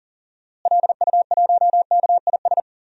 Morse code, heard as a keyed tone, about 37 wpm, 700 Hz. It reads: LU1KIS
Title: DXCC-100 Call Signs - 6 Characters - Repeated 2x - 40wpm